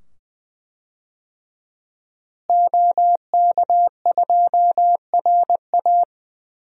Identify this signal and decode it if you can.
Morse code, heard as a keyed tone, about 20 wpm, 700 Hz. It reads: OK2RA
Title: DXCC-100 Call Signs - 5 Characters - Repeated 2x - 20wpm